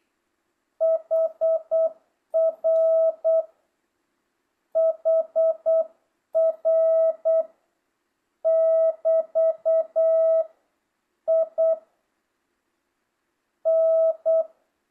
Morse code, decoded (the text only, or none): HR HR <BT> I N